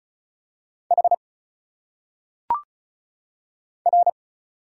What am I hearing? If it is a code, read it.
Morse code, heard as a keyed tone, about 36 wpm, 700 Hz. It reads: H R
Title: Single Letters - Phonetic Pronunciation 35wpm